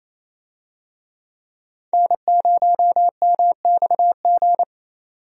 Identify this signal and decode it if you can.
Morse code, heard as a keyed tone, about 28 wpm, 700 Hz. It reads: N0MXG
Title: USA Call Signs - 5 Characters - 28wpm